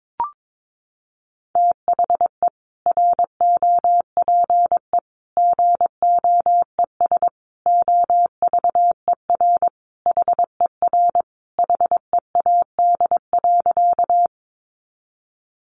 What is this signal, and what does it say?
Morse code, heard as a keyed tone, about 22 wpm, 700 Hz. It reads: THE ROPE GOES OVER HER HEAD.
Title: Easy Phrases 22wpm